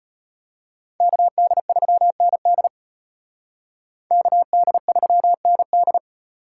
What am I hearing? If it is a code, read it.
Morse code, heard as a keyed tone, about 33 wpm, 700 Hz. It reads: KD3ND KD3ND